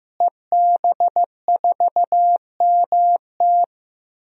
Morse code, heard as a keyed tone, about 15 wpm, 700 Hz. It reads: EB4MT
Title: DXCC-100 Call Signs - 5 Characters - Repeated 3x - 15wpm